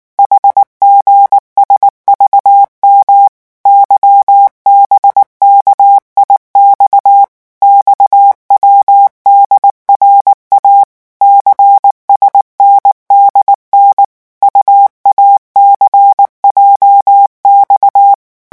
Morse code, heard as a keyed tone, about 19 wpm, 800 Hz. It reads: HGSVM YBKIX XWDRA CSNDN UACJX
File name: morse mp3 / revision7-vitesse 18 mots